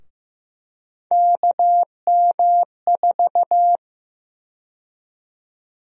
Morse code, heard as a keyed tone, about 15 wpm, 700 Hz. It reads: KM4
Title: USA Call Sign Prefixes - Repeated 2x - 15wpm